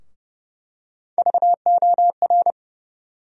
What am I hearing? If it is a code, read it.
Morse code, heard as a keyed tone, about 31 wpm, 700 Hz. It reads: VOR
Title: USA Call Sign Postfixes - 3 Letters 30wpm